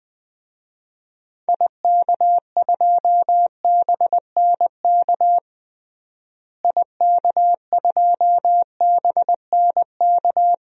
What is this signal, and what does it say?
Morse code, heard as a keyed tone, about 21 wpm, 700 Hz. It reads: IK2BNK IK2BNK
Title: DXCC-100 Call Signs - 6 Characters - Repeated 2x - 20wpm